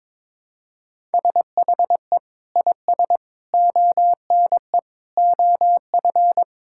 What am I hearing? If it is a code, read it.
Morse code, heard as a keyed tone, about 22 wpm, 700 Hz. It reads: SHE IS ONE OF
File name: practice / 22wpm / Sentences with 3-Word Phrases - 22wpm